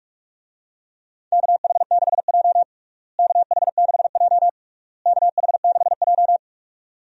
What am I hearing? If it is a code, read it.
Morse code, heard as a keyed tone, about 47 wpm, 700 Hz. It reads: KH6J KH6J KH6J
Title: USA Call Signs - 4 Characters - Repeated 3x - 45wpm